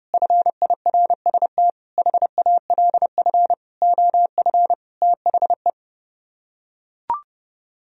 Morse code, heard as a keyed tone, about 32 wpm, 700 Hz. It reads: FIRST HALF OF THE